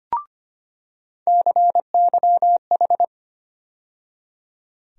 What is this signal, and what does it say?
Morse code, heard as a keyed tone, about 25 wpm, 700 Hz. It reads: CYH
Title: USA Call Sign Postfixes - 3 Letters 25wpm